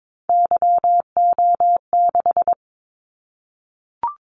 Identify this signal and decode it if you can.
Morse code, heard as a keyed tone, about 22 wpm, 700 Hz. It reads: YO6
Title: Callsign Prefixes for Top 25 DXCC - 22wpm